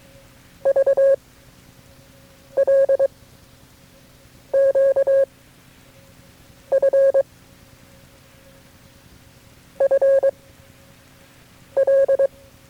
Morse code, heard as a keyed tone, about 23 wpm, 550 Hz. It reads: V L Q F F L
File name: FreeFiles / CW-practice / KB6MT-tape3-side1